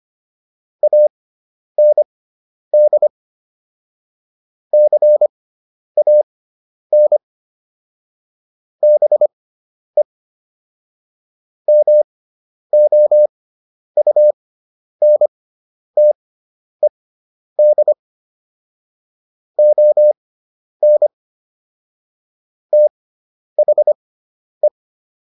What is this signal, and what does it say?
Morse code, heard as a keyed tone, about 25 wpm, spaced as for 10 wpm, 600 Hz. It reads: AND CAN BE MOUNTED ON THE